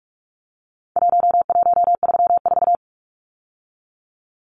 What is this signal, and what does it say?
Morse code, heard as a keyed tone, about 44 wpm, 700 Hz. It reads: 1134
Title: Numbers - Contest Style - 4-Digit 45wpm